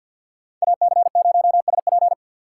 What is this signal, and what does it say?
Morse code, heard as a keyed tone, about 50 wpm, 690 Hz. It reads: AC0SP
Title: USA Call Signs - 5 Characters - 6x Character Spacing - 50wpm